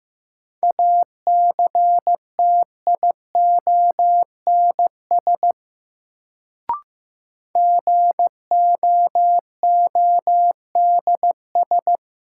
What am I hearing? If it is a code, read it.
Morse code, heard as a keyed tone, about 15 wpm, 700 Hz. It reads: ACTIONS GOODS
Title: Sentences from Top 1000 Words - Review Part 6 - New Word Drill - 15wpm